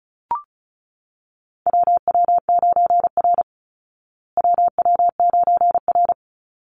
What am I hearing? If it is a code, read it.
Morse code, heard as a keyed tone, about 34 wpm, 700 Hz. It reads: WW9R WW9R
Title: Call Signs Repeated 3x - 35wpm